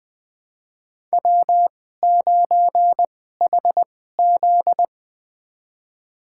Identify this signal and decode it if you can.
Morse code, heard as a keyed tone, about 20 wpm, spaced as for 15 wpm, 700 Hz. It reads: W9HZ
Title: USA Call Signs - 4 Characters - 2x Character Spacing - 20wpm